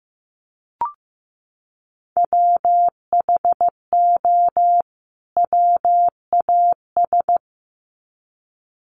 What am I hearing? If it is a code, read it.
Morse code, heard as a keyed tone, about 15 wpm, 700 Hz. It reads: WHO WAS
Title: Sets of 2 Words 15wpm